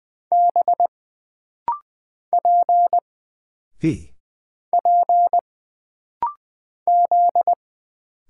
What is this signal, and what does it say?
Morse code, heard as a keyed tone, about 20 wpm, 700 Hz. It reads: B P P Z
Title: Single Character (Letter:Number:Punctuation:Prosign) - Rapid-Fire - 20wpm